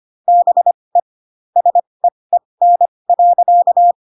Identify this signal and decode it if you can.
Morse code, heard as a keyed tone, about 25 wpm, spaced as for 20 wpm, 700 Hz. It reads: BE SEEN.
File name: Pinocchio / Pinocchio ch6 15 25 ik1hge